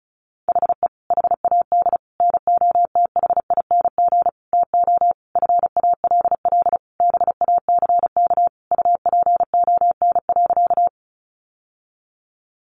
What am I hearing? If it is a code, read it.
Morse code, heard as a keyed tone, about 34 wpm, 700 Hz. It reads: HE HAD NOTHING TO FALL BACK UPON.
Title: Sentences from Top 1100 Words 35wpm